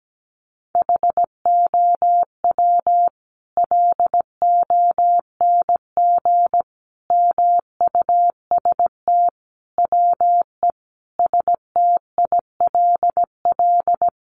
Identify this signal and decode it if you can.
Morse code, heard as a keyed tone, about 17 wpm, 700 Hz. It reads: HOW LONG MUST WE STILL
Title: Sentences from Top 1200 Words 17wpm